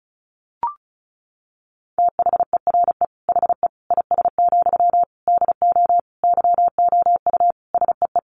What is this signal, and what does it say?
Morse code, heard as a keyed tone, about 34 wpm, 700 Hz. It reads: THERE HE IS, DO YOU SEE
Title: Sentences from Top 100 Words 35wpm